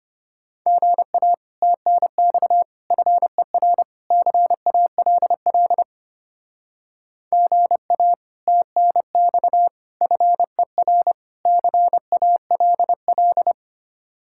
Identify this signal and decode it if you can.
Morse code, heard as a keyed tone, about 32 wpm, 700 Hz. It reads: GA TNX FER CALL GA TNX FER CALL